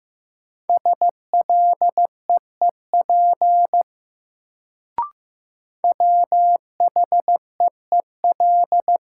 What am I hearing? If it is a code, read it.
Morse code, heard as a keyed tone, about 15 wpm, 700 Hz. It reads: SLEEP WHEEL
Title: Sentences from Top 500 Words - New Word Drill - 15wpm